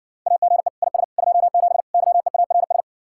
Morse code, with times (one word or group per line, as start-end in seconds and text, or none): 0.3-0.7 AGE
0.8-1.0 IS
1.2-1.8 27
1.9-2.8 YEARS